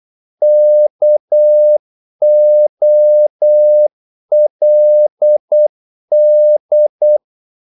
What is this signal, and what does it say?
Morse code, heard as a keyed tone, about 8 wpm, 600 Hz. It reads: KOLD
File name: Korte ord DK 8wpm